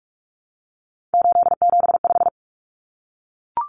Morse code, heard as a keyed tone, about 44 wpm, 700 Hz. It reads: 875
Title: Numbers - 3-Digit 45wpm